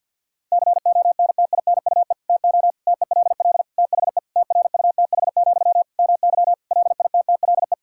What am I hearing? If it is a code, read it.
Morse code, heard as a keyed tone, about 52 wpm, 700 Hz. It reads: CONTINUE TO TELL THE TRUTH, MY LITTLE
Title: Sentences from Top 1000 Words - Review Part 6 - 50wpm